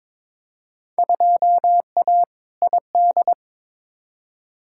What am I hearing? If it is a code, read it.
Morse code, heard as a keyed tone, about 22 wpm, 700 Hz. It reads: 2A ID
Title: Field Day Exchanges - 22wpm